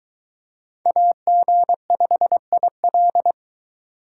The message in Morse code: AG5IL

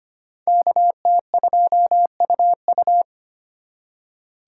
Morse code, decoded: KT2UU

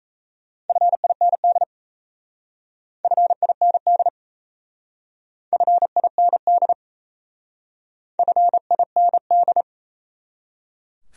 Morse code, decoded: FIND FIND FIND FIND

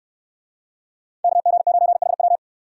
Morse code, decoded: KG9HW

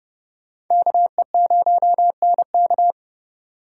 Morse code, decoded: KE0NK

